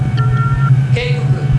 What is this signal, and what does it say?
Speech. There is no code.